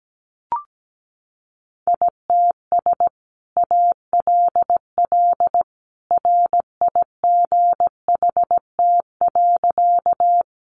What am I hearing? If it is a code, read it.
Morse code, heard as a keyed tone, about 17 wpm, 700 Hz. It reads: ITS ALL RIGHT.
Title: Sentences from Top 600 Words 17wpm